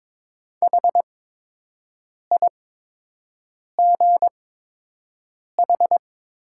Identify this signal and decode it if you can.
Morse code, heard as a keyed tone, about 22 wpm, spaced as for 6 wpm, 700 Hz. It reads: HIGH
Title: 8x Character Spacing - Top 200 Words - 22wpm